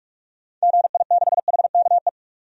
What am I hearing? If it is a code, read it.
Morse code, heard as a keyed tone, about 41 wpm, 700 Hz. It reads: GI6HKE
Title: DXCC-100 Call Signs - 6 Characters - 45wpm